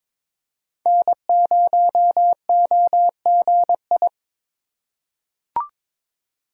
Morse code, heard as a keyed tone, about 23 wpm, 700 Hz. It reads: N0OGI
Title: USA Call Signs - 5 Characters - 4x Character Spacing - 22wpm